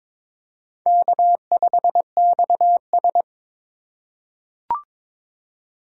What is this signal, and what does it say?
Morse code, heard as a keyed tone, about 22 wpm, 700 Hz. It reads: K5XS
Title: USA Call Signs - 4 Characters - 2x Character Spacing - 22wpm